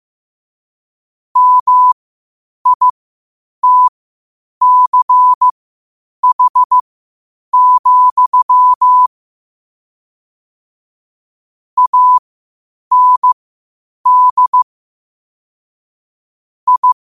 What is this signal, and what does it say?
Morse code, heard as a keyed tone, about 15 wpm, spaced as for 8 wpm, 1000 Hz. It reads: MITCH, AND I
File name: qotdfiles / 8wpm